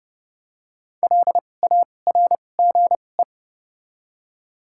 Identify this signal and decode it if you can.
Morse code, heard as a keyed tone, about 30 wpm, spaced as for 22 wpm, 700 Hz. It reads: LARGE
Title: 2x Character Spacing - Top 200 Words - 30wpm